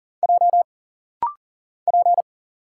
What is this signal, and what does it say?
Morse code, heard as a keyed tone, about 40 wpm, 700 Hz. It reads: J P
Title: Single Letter-Number Alt - Rapid-Fire 40wpm